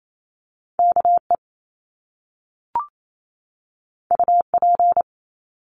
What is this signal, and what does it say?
Morse code, heard as a keyed tone, about 28 wpm, 700 Hz. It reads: KE UP
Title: Top Two Letter Combinations in Words 28wpm